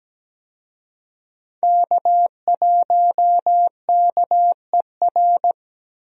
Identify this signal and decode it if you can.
Morse code, heard as a keyed tone, about 17 wpm, 700 Hz. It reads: K1KER
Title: USA Call Signs - 5 Characters - 17wpm